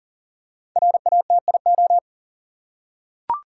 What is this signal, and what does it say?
Morse code, heard as a keyed tone, about 40 wpm, 700 Hz. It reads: RATIO
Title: Five Letter Words - 40wpm